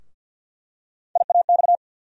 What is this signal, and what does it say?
Morse code, heard as a keyed tone, about 49 wpm, 700 Hz. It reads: IAX
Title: USA Call Sign Postfixes - 3 Letters 50wpm